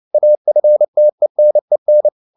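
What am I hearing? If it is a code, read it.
Morse code, heard as a keyed tone, about 29 wpm, 600 Hz. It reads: AFTENEN
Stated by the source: Fast (QRQ) Morse at 29 wpm, from CW med Gnister.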